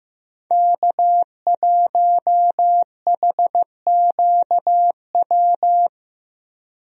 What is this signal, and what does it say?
Morse code, heard as a keyed tone, about 15 wpm, 700 Hz. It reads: K1HQW
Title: USA Call Signs - 5 Characters - 15wpm